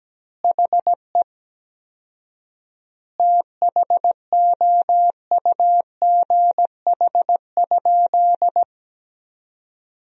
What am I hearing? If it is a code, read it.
Morse code, heard as a keyed tone, about 17 wpm, 700 Hz. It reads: HE THOUGH?
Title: 4x Word-Spacing - Sentences from Top 400 Words - 17wpm